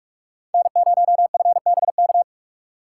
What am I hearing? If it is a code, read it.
Morse code, heard as a keyed tone, about 45 wpm, 700 Hz. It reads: N0FBK
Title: USA Call Signs - 5 Characters - 2x Character Spacing - 45wpm